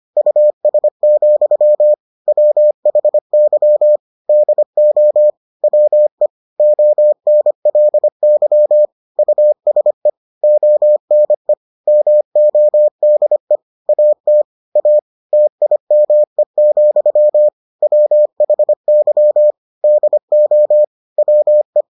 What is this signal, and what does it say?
Morse code, heard as a keyed tone, about 25 wpm, 600 Hz. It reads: US, WHY DO WE ONLY USE ONE MODE AT A TIME, WHY DO WE